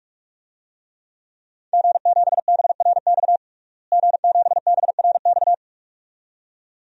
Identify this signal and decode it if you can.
Morse code, heard as a keyed tone, about 42 wpm, 700 Hz. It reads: G7BRX G7BRX
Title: DXCC-100 Call Signs - 5 Characters - Repeated 2x - 45wpm